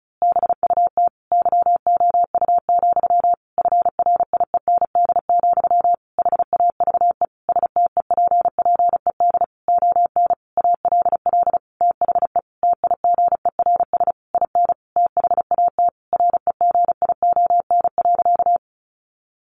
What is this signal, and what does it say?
Morse code, heard as a keyed tone, about 34 wpm, 700 Hz. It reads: BUT YOU, FRIEND, HAVE STEPPED ON ALL THE TIGERS IN THAT REGION.
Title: A Sound Of Thunder 35wpm